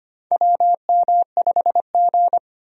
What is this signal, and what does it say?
Morse code, heard as a keyed tone, about 25 wpm, 700 Hz. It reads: WM5G
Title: USA Call Signs - 4 Characters - 25wpm